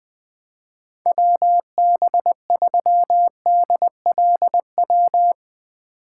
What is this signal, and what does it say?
Morse code, heard as a keyed tone, about 20 wpm, 700 Hz. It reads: WB3DLW
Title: USA Call Signs - 6 Characters - 4x Character Spacing - 20wpm